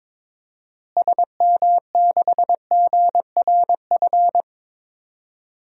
Morse code, heard as a keyed tone, about 22 wpm, 700 Hz. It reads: SM6GRF
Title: DXCC-100 Call Signs - 6 Characters - Repeated 2x - 22wpm